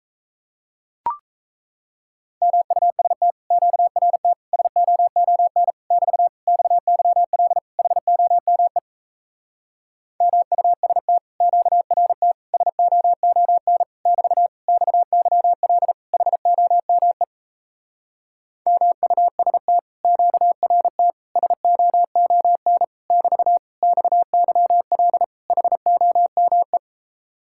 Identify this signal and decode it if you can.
Morse code, heard as a keyed tone, about 38 wpm, 700 Hz. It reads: MUST QRT SOON <BT> XYL HOME MUST QRT SOON <BT> XYL HOME MUST QRT SOON <BT> XYL HOME